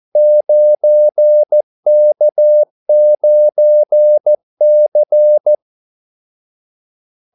Morse code, heard as a keyed tone, about 14 wpm, 600 Hz. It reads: 9K9C
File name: Callsigns-14wpm